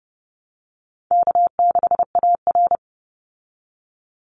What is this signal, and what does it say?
Morse code, heard as a keyed tone, about 30 wpm, 700 Hz. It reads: K6AR
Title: Easy Call Signs Repeated 3x - 30wpm